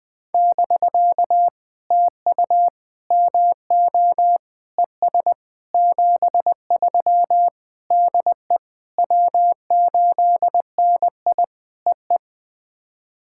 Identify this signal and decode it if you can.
Morse code, heard as a keyed tone, about 19 wpm, 700 Hz. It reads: <BK> TU MO ES 73 DE W8NI EE
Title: POTA QSO Practice - 20wpm